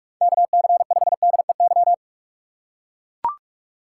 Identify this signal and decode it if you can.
Morse code, heard as a keyed tone, about 44 wpm, 700 Hz. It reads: KC5DEY
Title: USA Call Signs - 6 Characters - 2x Character Spacing - 45wpm